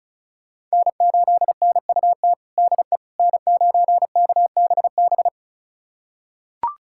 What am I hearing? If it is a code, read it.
Morse code, heard as a keyed tone, about 34 wpm, 700 Hz. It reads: N8NUT DE N9KBB
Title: USA Callsign DE Callsign - 5 Characters - 35wpm